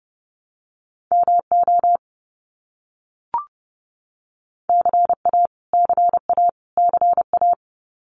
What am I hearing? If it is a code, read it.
Morse code, heard as a keyed tone, about 30 wpm, 700 Hz. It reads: MO CA CA CA
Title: US States Repeated 3x 30wpm